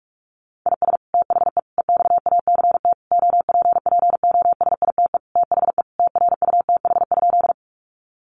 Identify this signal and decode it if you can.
Morse code, heard as a keyed tone, about 44 wpm, 700 Hz. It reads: IS THE EXACT OPPOSITE THE TRUTH?
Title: Sentences from Top 1300 Words 45wpm